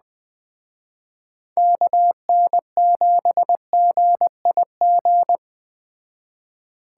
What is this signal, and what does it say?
Morse code, heard as a keyed tone, about 21 wpm, 700 Hz. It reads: KN7GIG